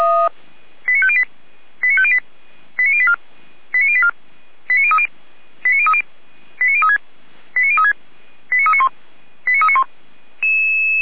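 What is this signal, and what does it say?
Five-tone selective call (ZVEI), heard as quick runs of five tones, tones of about 70 ms. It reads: 89508 89508 89084 89084 89030 89030 89036 89036 89391 89391
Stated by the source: ZVEI five-tone selective call 89042, received on SDR.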